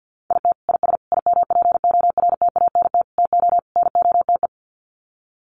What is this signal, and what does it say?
Morse code, heard as a keyed tone, about 50 wpm, 700 Hz. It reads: IT IS IMPORTANT TO NOTE
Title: Sets of 5 Words 50wpm